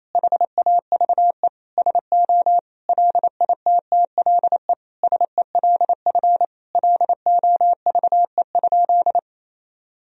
Morse code, heard as a keyed tone, about 27 wpm, 700 Hz. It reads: HAVE SO LITTLE SELF LOVE?
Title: Sentences from Top 1000 Words - Review Part 1 - 28wpm